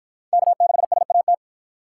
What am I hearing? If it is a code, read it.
Morse code, heard as a keyed tone, about 53 wpm, 700 Hz. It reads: K6SAT